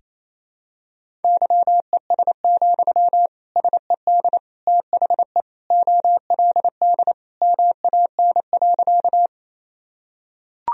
Morse code, heard as a keyed tone, about 28 wpm, 700 Hz. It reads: YES, SED THE OLD MAN.
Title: Sentences from Top 500 Words and Abbreviations 28wpm